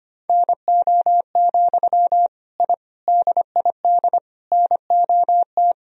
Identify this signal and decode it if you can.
Morse code, heard as a keyed tone, about 25 wpm, 700 Hz. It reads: NO, I DID NOT
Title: Sentences from Top 1000 Words - Review Part 2 - 25wpm